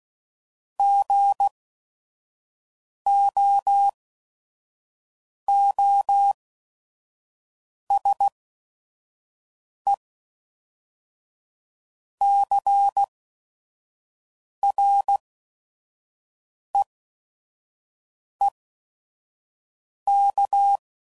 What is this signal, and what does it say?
Morse code, heard as a keyed tone, about 16 wpm, spaced as for 5 wpm, 800 Hz. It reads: GOOSE CREEK
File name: CW Code Groups @5WPM / fromSM's / Cities